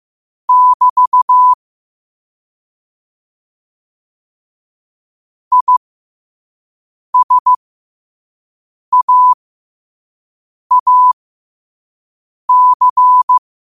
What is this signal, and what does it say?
Morse code, heard as a keyed tone, about 15 wpm, spaced as for 5 wpm, 1000 Hz. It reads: <BT> ISAAC
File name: qotdfiles / 5wpm